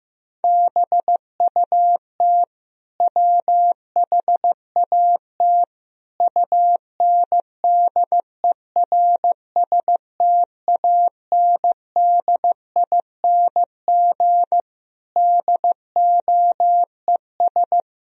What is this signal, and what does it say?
Morse code, heard as a keyed tone, about 16 wpm, 700 Hz. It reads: BUT WHAT UNDERSTANDING DOES
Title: Sentences from Top 1000 Words - Review Part 3 - 15wpm